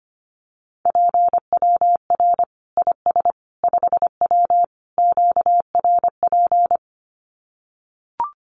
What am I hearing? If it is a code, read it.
Morse code, heard as a keyed tone, about 25 wpm, 700 Hz. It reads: PWR IS 5W QRP
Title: Common QSO Phrases 25wpm